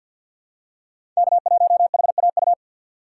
Morse code, heard as a keyed tone, about 47 wpm, 700 Hz. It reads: K1HAU
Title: USA Call Signs - 5 Characters - 4x Character Spacing - 50wpm